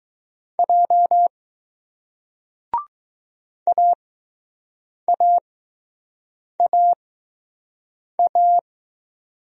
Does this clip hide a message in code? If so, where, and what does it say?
Yes: Morse code from 0.6 to 8.6 s, heard as a keyed tone, about 20 wpm, spaced as for 6 wpm, 700 Hz. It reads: J AAAA